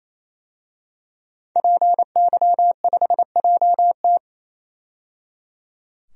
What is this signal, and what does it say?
Morse code, heard as a keyed tone, about 28 wpm, 700 Hz. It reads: PY5JT
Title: DXCC-100 Call Signs - 5 Characters - 28wpm